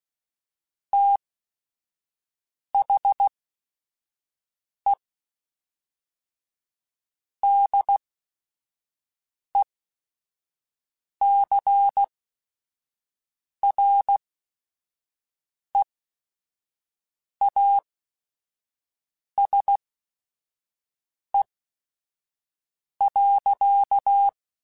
Morse code, heard as a keyed tone, about 16 wpm, spaced as for 5 wpm, 800 Hz. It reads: THE DECREASE.
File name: Technical Subjects / Technical Subjects-mp3s / XC @5